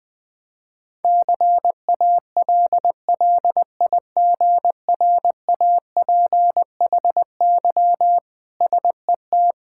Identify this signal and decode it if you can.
Morse code, heard as a keyed tone, about 20 wpm, 700 Hz. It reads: CALLIGRAPHY SET
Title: Difficult Words with Hints - 20wpm